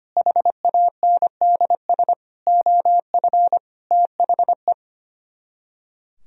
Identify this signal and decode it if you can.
Morse code, heard as a keyed tone, about 30 wpm, 700 Hz. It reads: HANDS OF THE
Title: Speed-Racing - Sets of 5 Words 25wpm